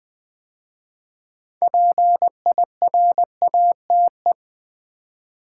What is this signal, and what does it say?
Morse code, heard as a keyed tone, about 20 wpm, 700 Hz. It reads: PIRATE
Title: 6 Letter Words - First 3 Letters Given - 20wpm